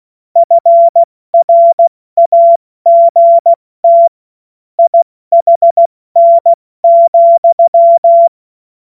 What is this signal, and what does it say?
Morse code, heard as a keyed tone, about 16 wpm, 680 Hz. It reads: FRAGT IHN,
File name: Morsekurs / Witze / Witz 06. April 2021, 70 F80